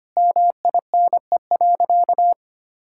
Morse code, heard as a keyed tone, about 25 wpm, 700 Hz. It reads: MINE.
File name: practice / 25wpm / Sentences with 3-Word Phrases - 25wpm